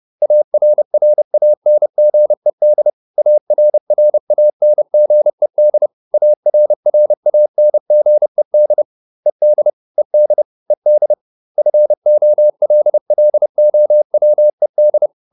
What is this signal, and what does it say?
Morse code, heard as a keyed tone, about 30 wpm, 600 Hz. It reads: ARRANGED ARRANGED ARRANGED ED ED ED FOLLOWED